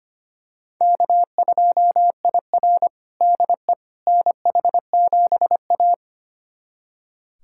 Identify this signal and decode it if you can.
Morse code, heard as a keyed tone, about 25 wpm, 700 Hz. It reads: K2IR DE NH7A
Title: USA Callsign DE Callsign - 4 Characters - 25wpm